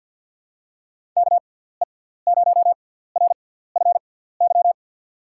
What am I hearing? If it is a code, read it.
Morse code, heard as a keyed tone, about 47 wpm, spaced as for 17 wpm, 700 Hz. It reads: KE0RFY
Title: USA Call Signs - 6 Characters - 6x Character Spacing - 50wpm